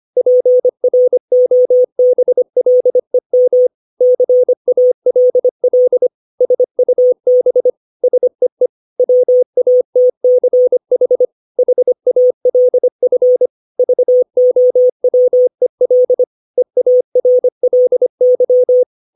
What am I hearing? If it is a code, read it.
Morse code, heard as a keyed tone, about 25 wpm, 500 Hz. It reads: PROBLEM CALL SUB SEE WATCH HALF VOWEL EARLY
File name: 632 words / 632 Words 25WPM